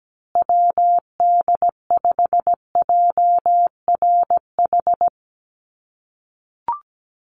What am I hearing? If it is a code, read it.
Morse code, heard as a keyed tone, about 17 wpm, 700 Hz. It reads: WD5JRH
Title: USA Call Signs Repeated 2x 17wpm